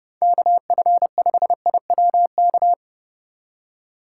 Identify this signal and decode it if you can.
Morse code, heard as a keyed tone, about 30 wpm, 700 Hz. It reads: KF5IWK